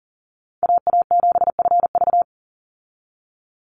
Morse code, heard as a keyed tone, about 40 wpm, 700 Hz. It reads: AA7FV
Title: Call Signs Repeated 3x - 40wpm